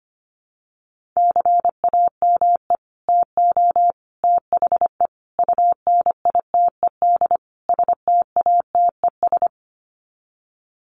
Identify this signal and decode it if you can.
Morse code, heard as a keyed tone, about 25 wpm, 700 Hz. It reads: CAME TO THE UNITED STATES
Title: Sets of 5 Words Encore2 25wpm